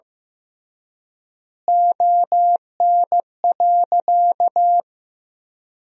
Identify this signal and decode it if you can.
Morse code, heard as a keyed tone, about 15 wpm, 700 Hz. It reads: ON.
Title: 3x Word-Spacing - Sentences from Top 200 Words - 15wpm